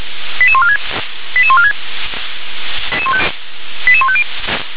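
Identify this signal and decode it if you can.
Five-tone selective call (ZVEI), heard as quick runs of five tones, tones of about 70 ms. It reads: 88146 88146 88150
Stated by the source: ZVEI five-tone selective call received on an SDR.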